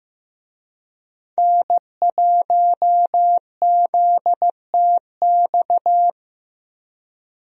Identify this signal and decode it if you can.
Morse code, heard as a keyed tone, about 15 wpm, 700 Hz. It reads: N1ZTX
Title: USA Call Signs - 5 Characters - Repeated 3x - 15wpm